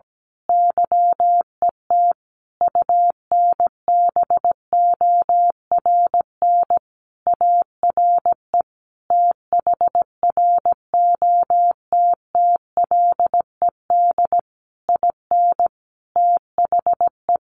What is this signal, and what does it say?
Morse code, heard as a keyed tone, about 17 wpm, 700 Hz. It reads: YET UNBORN ARE THROTTLED IN THE
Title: A Sound Of Thunder 17wpm